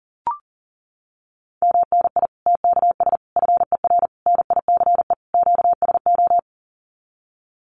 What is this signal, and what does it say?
Morse code, heard as a keyed tone, about 40 wpm, 700 Hz. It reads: MNI TKS FER NICE QSO
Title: Common QSO Phrases 40wpm